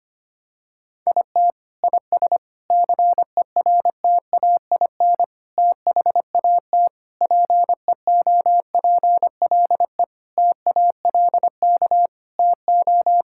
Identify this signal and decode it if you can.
Morse code, heard as a keyed tone, about 25 wpm, 700 Hz. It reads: IT IS CERTAIN THAT PEOPLE TALK TO
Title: Sentences from Top 1000 Words - Review Part 5 - 25wpm